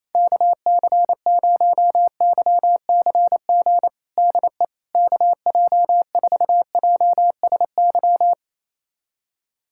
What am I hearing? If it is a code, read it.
Morse code, heard as a keyed tone, about 28 wpm, 700 Hz. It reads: KC0YCG DE KJ4JSY
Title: USA Callsign DE Callsign - 6 Characters - 28wpm